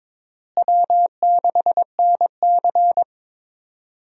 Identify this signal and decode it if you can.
Morse code, heard as a keyed tone, about 22 wpm, 700 Hz. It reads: W6NC
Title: USA Call Signs - 4 Characters - Repeated 3x - 22wpm